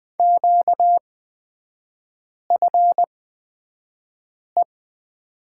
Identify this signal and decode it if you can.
Morse code, heard as a keyed tone, about 20 wpm, spaced as for 5 wpm, 700 Hz. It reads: QFE
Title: 236 Lesson 38e - Letter q - Three Characters 20wpm